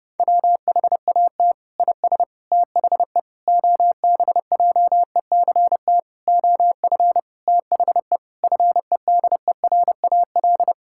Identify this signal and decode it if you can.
Morse code, heard as a keyed tone, about 32 wpm, 700 Hz. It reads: WHAT IS THE OBJECT OF THE FEDERAL
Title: Sentences from Top 1000 Words - Review Part 5 - 30wpm